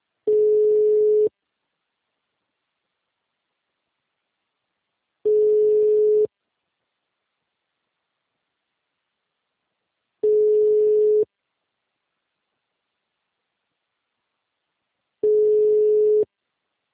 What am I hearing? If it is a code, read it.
Ringback tone (Europe, 425 Hz).
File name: out- -2030- - - .6